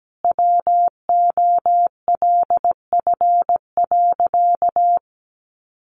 Morse code, heard as a keyed tone, about 17 wpm, 700 Hz. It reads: WOLF.
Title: Additional Easy Phrases 17wpm